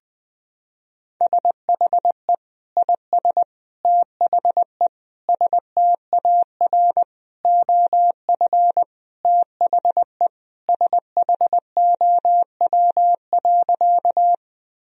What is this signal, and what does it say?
Morse code, heard as a keyed tone, about 21 wpm, 700 Hz. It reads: SHE IS THE STAR OF THE SHOW.
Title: Sentences from Top 1000 Words - Review Part 2 - 20wpm